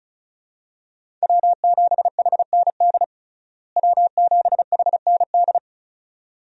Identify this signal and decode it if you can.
Morse code, heard as a keyed tone, about 36 wpm, 700 Hz. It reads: W7HND W7HND
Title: USA Call Signs - 5 Characters - Repeated 2x - 35wpm